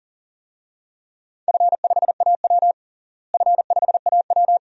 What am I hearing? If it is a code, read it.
Morse code, heard as a keyed tone, about 37 wpm, 700 Hz. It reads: F5AW F5AW
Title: DXCC-100 Call Signs - 4 Characters - Repeated 2x - 40wpm